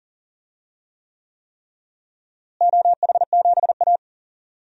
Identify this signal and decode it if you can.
Morse code, heard as a keyed tone, about 40 wpm, 700 Hz. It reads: OH7A
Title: DXCC-100 Call Signs - 4 Characters - Repeated 3x - 40wpm